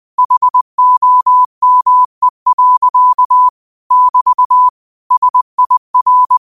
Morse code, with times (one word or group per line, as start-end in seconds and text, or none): 0.2-3.5 HOME.
3.9-4.7 <BT>
5.1-6.4 SIR